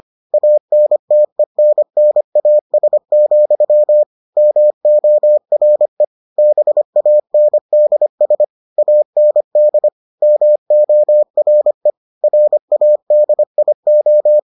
Morse code, heard as a keyed tone, about 25 wpm, 600 Hz. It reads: ANTENNAS, MORE BANDS AND MORE RADIO